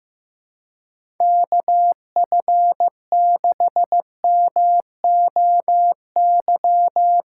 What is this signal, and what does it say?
Morse code, heard as a keyed tone, about 15 wpm, 700 Hz. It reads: KF6MOY
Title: USA Call Signs - 6 Characters - 4x Character Spacing - 15wpm